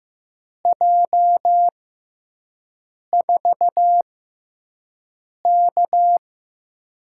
Morse code, heard as a keyed tone, about 15 wpm, spaced as for 5 wpm, 700 Hz. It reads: J4K
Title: USA Call Signs - 4 Characters - 6x Character Spacing - 15wpm